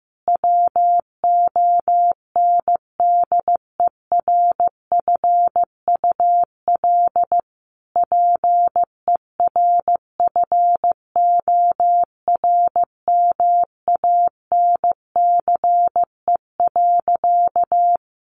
Morse code, heard as a keyed tone, about 15 wpm, 700 Hz. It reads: WONDERFUL PERFORMANCE.
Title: Sentences from Top 1100 Words 15wpm